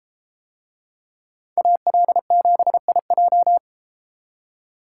Morse code, heard as a keyed tone, about 32 wpm, 700 Hz. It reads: AL7IJ